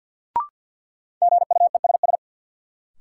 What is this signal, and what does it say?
Morse code, heard as a keyed tone, about 53 wpm, 700 Hz. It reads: GUESS
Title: Sentences from Top 1500 Words - New Word Drill - 50wpm